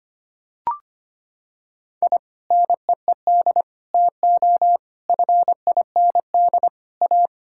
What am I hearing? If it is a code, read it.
Morse code, heard as a keyed tone, about 25 wpm, 700 Hz. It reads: I NEED TO FIND A
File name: practice / 25wpm / Sentences with 3-Word Phrases - 25wpm